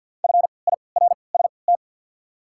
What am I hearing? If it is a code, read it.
Morse code, heard as a keyed tone, about 54 wpm, spaced as for 29 wpm, 690 Hz. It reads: FIRST
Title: 3x Character Spacing - Top 100 Words - 50wpm